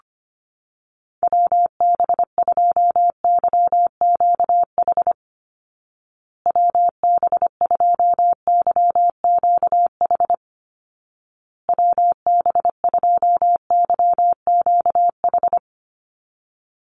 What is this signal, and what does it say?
Morse code, heard as a keyed tone, about 25 wpm, 700 Hz. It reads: WB2YQH WB2YQH WB2YQH
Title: Call Signs Repeated 2x - 25wpm